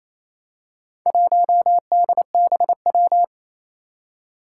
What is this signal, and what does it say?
Morse code, heard as a keyed tone, about 29 wpm, 700 Hz. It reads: 1DBW
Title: Four Challenging Letter-Number Combinations 28wpm